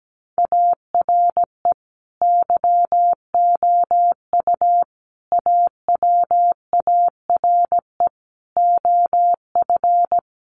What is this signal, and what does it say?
Morse code, heard as a keyed tone, about 17 wpm, 700 Hz. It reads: ARE YOU AWARE OF
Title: Sentences from Top 1200 Words 17wpm